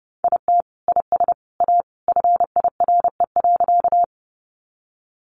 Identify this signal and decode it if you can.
Morse code, heard as a keyed tone, about 30 wpm, 700 Hz. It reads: IT IS A FIRE.
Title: Sentences from Top 400 Words 30wpm